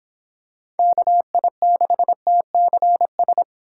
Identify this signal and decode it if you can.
Morse code, heard as a keyed tone, about 25 wpm, 700 Hz. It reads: KI6TCS